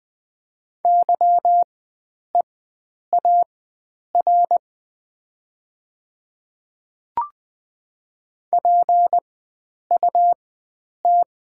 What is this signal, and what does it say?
Morse code, heard as a keyed tone, about 20 wpm, spaced as for 9 wpm, 700 Hz. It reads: YEAR PUT
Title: 4x Character Spacing - Top 300 Words - 20wpm